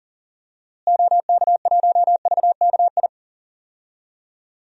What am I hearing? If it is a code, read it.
Morse code, heard as a keyed tone, about 40 wpm, 700 Hz. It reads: OK1VKI
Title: DXCC-100 Call Signs - 6 Characters - Repeated 2x - 40wpm